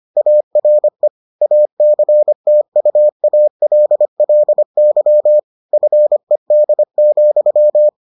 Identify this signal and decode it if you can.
Morse code, heard as a keyed tone, about 25 wpm, 600 Hz. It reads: ARE ACTUALLY FED,